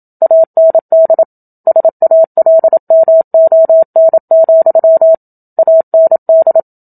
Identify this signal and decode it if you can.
Morse code, heard as a keyed tone, about 27 wpm, 650 Hz. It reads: AND SALMON, AND
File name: Book Two / 01-Chapter 1 - 27 WPM